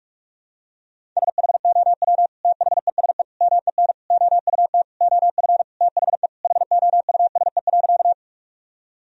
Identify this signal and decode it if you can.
Morse code, heard as a keyed tone, about 46 wpm, 700 Hz. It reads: SHOW THESE MEN OUT OF THE HOUSE.
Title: Sentences from Top 1000 Words - Review Part 1 - 45wpm